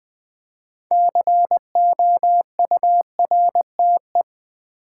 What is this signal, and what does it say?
Morse code, heard as a keyed tone, about 20 wpm, 700 Hz. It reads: COURTE
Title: Build A Word - 3 Letters At A Time - 8-Letter Words - 20wpm